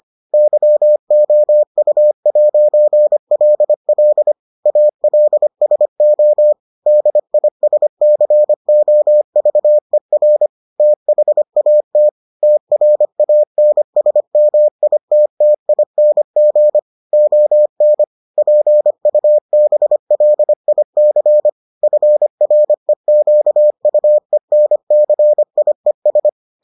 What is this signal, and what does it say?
Morse code, heard as a keyed tone, about 25 wpm, 600 Hz. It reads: YOU'LL ALSO DISCOVER THAT TRANSMITTING ON PUBLIC FREQUENCIES